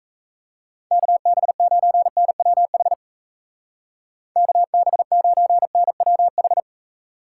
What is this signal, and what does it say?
Morse code, heard as a keyed tone, about 38 wpm, 700 Hz. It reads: KB9NWH KB9NWH